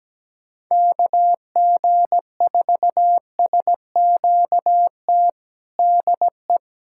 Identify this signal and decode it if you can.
Morse code, heard as a keyed tone, about 17 wpm, 700 Hz. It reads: KG4SQT DE
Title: USA Callsign DE Callsign - 6 Characters - 17wpm